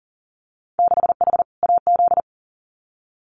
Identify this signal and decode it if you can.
Morse code, heard as a keyed tone, about 40 wpm, 700 Hz. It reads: 6H AZ
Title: Winter Field Day Exchanges Encore 40wpm